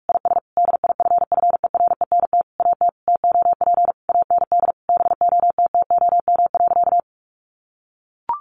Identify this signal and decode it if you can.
Morse code, heard as a keyed tone, about 44 wpm, 700 Hz. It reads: IS DIFFERENT AT TOP AND BOTTOM.
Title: Sentences from Top 1400 Words 45wpm